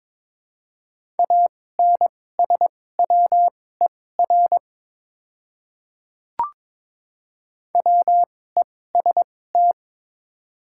Morse code, heard as a keyed tone, about 22 wpm, spaced as for 16 wpm, 700 Hz. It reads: ANSWER WEST
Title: 2x Character Spacing - Top 500 Words - 22wpm